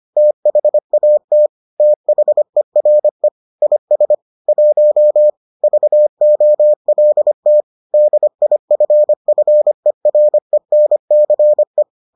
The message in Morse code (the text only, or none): THAT THERE IS 1 VOLT DIFFERENCE